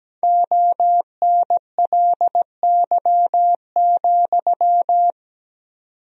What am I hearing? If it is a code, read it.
Morse code, heard as a keyed tone, about 17 wpm, 700 Hz. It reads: ONLY,
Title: 4x Word-Spacing - Sentences from Top 300 Words - 17wpm